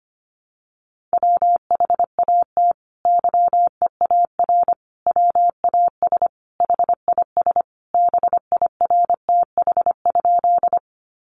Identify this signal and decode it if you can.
Morse code, heard as a keyed tone, about 25 wpm, 700 Hz. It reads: WHAT YEAR WAS HIS BIRTH?
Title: Sentences from Top 800 Words 25wpm